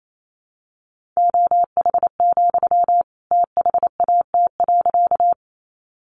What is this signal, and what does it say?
Morse code, heard as a keyed tone, about 28 wpm, 700 Hz. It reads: OH, THAT.
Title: Sentences from Top 500 Words 28wpm